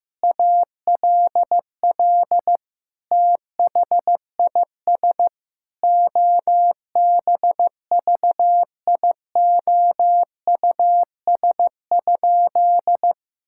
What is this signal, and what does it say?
Morse code, heard as a keyed tone, about 15 wpm, 700 Hz. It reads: ALL THIS OBVIOUS?
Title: Sentences from Top 1600 Words 15wpm